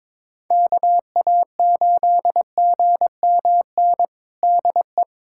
Morse code, heard as a keyed tone, about 22 wpm, 700 Hz. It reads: KA8GMN DE
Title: USA Callsign DE Callsign 22wpm